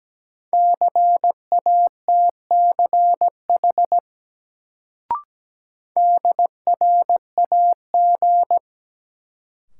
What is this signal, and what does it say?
Morse code, heard as a keyed tone, about 17 wpm, 700 Hz. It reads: CATCH DRAG
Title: Sentences from Top 1500 Words - New Word Drill - 17wpm